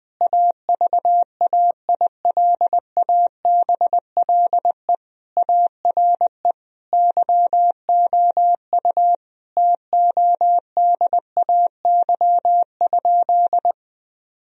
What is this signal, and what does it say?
Morse code, heard as a keyed tone, about 20 wpm, 700 Hz. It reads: AVAILABLE ARE YOU TODAY?
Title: Sentences from Top 1000 Words - Review Part 3 - 20wpm